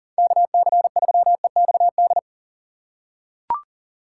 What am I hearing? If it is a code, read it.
Morse code, heard as a keyed tone, about 40 wpm, 700 Hz. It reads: KC3EXD
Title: USA Call Signs - 3x Character Spacing - 40wpm